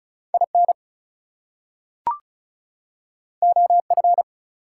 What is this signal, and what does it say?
Morse code, heard as a keyed tone, about 36 wpm, 700 Hz. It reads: IN OF
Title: Two Letter Words - 35wpm